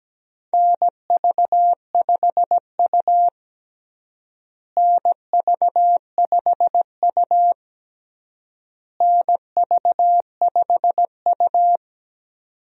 Morse code, heard as a keyed tone, about 17 wpm, 700 Hz. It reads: NV5U NV5U NV5U
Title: USA Call Signs - 4 Characters - Repeated 3x - 17wpm